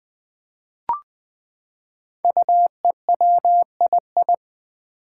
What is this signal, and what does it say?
Morse code, heard as a keyed tone, about 20 wpm, 700 Hz. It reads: UEWII
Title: Five Challenging Letter-Number Combinations 20wpm